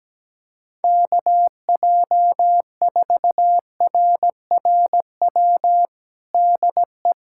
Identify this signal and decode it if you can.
Morse code, heard as a keyed tone, about 17 wpm, 700 Hz. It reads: KJ4RRW DE
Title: USA Callsign DE Callsign - 6 Characters - 17wpm